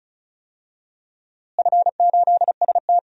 Morse code, heard as a keyed tone, about 36 wpm, 700 Hz. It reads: F8ST
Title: DXCC-100 Call Signs - 4 Characters - Repeated 3x - 35wpm